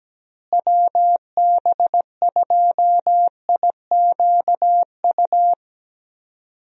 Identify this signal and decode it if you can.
Morse code, heard as a keyed tone, about 17 wpm, 700 Hz. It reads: WB2IQU
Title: USA Call Signs - 6 Characters - Repeated 3x - 17wpm